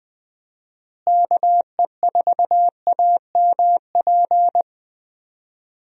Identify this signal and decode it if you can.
Morse code, heard as a keyed tone, about 21 wpm, 700 Hz. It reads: KE4AMP
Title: USA Call Signs - 6 Characters - Repeated 2x - 20wpm